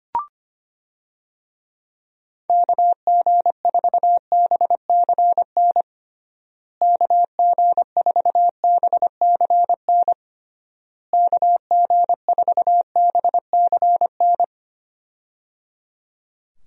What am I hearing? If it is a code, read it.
Morse code, heard as a keyed tone, about 25 wpm, 700 Hz. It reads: KG4BCN KG4BCN KG4BCN
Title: USA Call Signs - 6 Characters - Repeated 3x - 25wpm